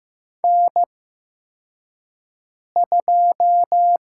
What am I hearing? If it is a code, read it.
Morse code, heard as a keyed tone, about 15 wpm, 700 Hz. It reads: N 2
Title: USA Call Signs - 5 Characters - 8x Character Spacing - 15wpm